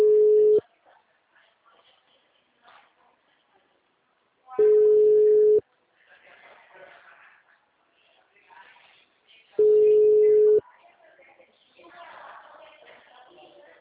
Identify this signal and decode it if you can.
Ringback tone (Europe, 425 Hz), with speech.
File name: out- -2006- - - .63206